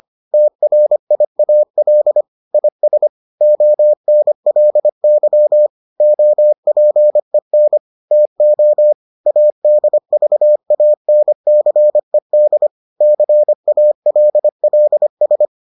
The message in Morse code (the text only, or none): TRIAL IS ONLY OPEN TO ADVANCED CALLS